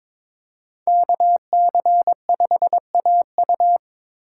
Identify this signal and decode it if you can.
Morse code, heard as a keyed tone, about 22 wpm, 700 Hz. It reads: KC5AU
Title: USA Call Signs - 5 Characters - 4x Character Spacing - 22wpm